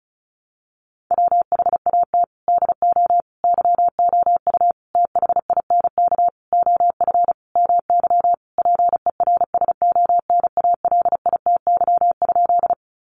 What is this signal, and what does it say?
Morse code, heard as a keyed tone, about 34 wpm, 700 Hz. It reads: WHAT DO YOU THINK OF MY PERSONALITY?
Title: Sentences from Top 1400 Words 35wpm